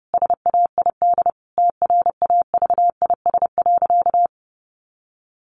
Morse code, heard as a keyed tone, about 30 wpm, 700 Hz. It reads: SAID TRAVIS.
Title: A Sound Of Thunder 30wpm